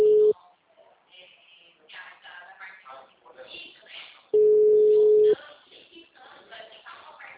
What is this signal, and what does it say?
Ringback tone (Europe, 425 Hz), with speech.
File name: out- -2003- - - .13465